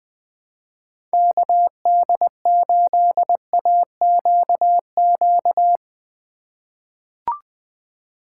Morse code, heard as a keyed tone, about 20 wpm, 700 Hz. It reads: KD8AQQ
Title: USA Call Signs - 6 Characters - 4x Character Spacing - 20wpm